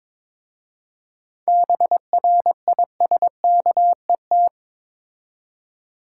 Morse code, heard as a keyed tone, about 21 wpm, 700 Hz. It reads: BRISKET
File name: Difficult Words with Hints - 22wpm